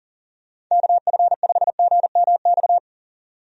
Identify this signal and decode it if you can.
Morse code, heard as a keyed tone, about 40 wpm, 700 Hz. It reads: KF5GMX
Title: USA Call Signs - 6 Characters - 40wpm